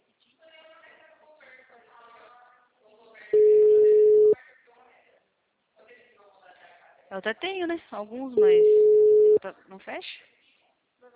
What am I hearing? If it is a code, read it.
Ringback tone (Europe, 425 Hz), with speech.
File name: out- -2013- - - .42793